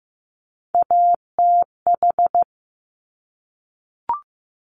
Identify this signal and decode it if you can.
Morse code, heard as a keyed tone, about 15 wpm, 700 Hz. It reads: ATH
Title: Top Three Letter Combinations in Words 15wpm